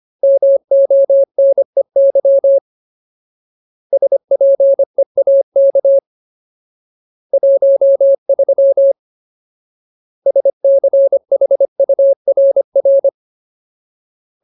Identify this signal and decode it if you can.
Morse code, heard as a keyed tone, about 25 wpm, 550 Hz. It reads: MONEY SPEAK 13 SCHURR
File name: Ultimate Word List / 25WPM Ultimate